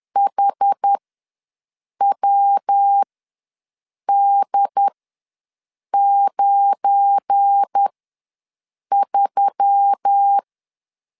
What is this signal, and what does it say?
Morse code, heard as a keyed tone, about 10 wpm, spaced as for 6 wpm, 800 Hz. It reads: HWD93
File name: LEZIONI / 9 28 ws 10wpm 4124